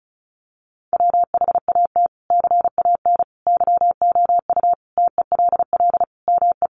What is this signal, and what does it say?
Morse code, heard as a keyed tone, about 34 wpm, 700 Hz. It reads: WHAT CAN YOU TELL ME
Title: Sentences from Top 1500 Words 35wpm